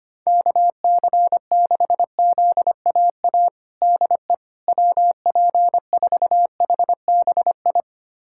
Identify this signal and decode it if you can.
Morse code, heard as a keyed tone, about 25 wpm, 700 Hz. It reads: KC6ZAA DE WP4HBI
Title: USA Callsign DE Callsign - 6 Characters - 25wpm